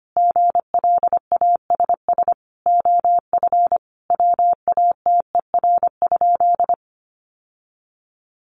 Morse code, heard as a keyed tone, about 25 wpm, 700 Hz. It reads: GLASS OF WATER?
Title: Sentences from Top 1300 Words 25wpm